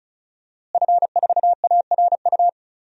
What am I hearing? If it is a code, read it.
Morse code, heard as a keyed tone, about 36 wpm, 700 Hz. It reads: F4ARU
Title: DXCC-100 Call Signs - 5 Characters - Repeated 3x - 35wpm